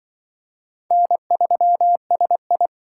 Morse code, heard as a keyed tone, about 24 wpm, 700 Hz. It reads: N3SI
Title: Speed Racing - USA Call Signs 20wpm